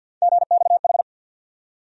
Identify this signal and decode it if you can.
Morse code, heard as a keyed tone, about 53 wpm, 700 Hz. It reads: GXH
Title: USA Call Sign Postfixes - 3 Letters 50wpm